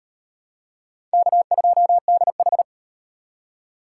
Morse code, heard as a keyed tone, about 33 wpm, 700 Hz. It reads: K2DH